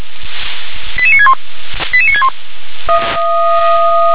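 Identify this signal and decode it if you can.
Five-tone selective call (ZVEI), heard as quick runs of five tones, tones of about 70 ms. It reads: 88961 88961
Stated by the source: ZVEI five-tone selective call 88460 received by SDR.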